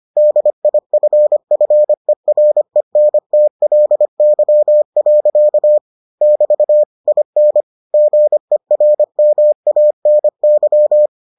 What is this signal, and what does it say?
Morse code, heard as a keyed tone, about 25 wpm, 600 Hz. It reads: DIFFERENTLY. <BT> IN GERMANY